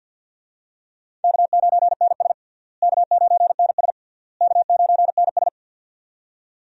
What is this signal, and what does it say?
Morse code, heard as a keyed tone, about 52 wpm, 700 Hz. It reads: K9NS K9NS K9NS
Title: USA Call Signs - 4 Characters - Repeated 3x - 50wpm